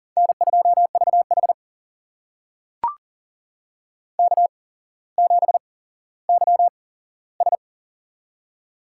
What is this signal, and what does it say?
Morse code, heard as a keyed tone, about 40 wpm, spaced as for 17 wpm, 690 Hz. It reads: N2VH K7YS
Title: USA Call Signs - 4 Characters - 8x Character Spacing - 40wpm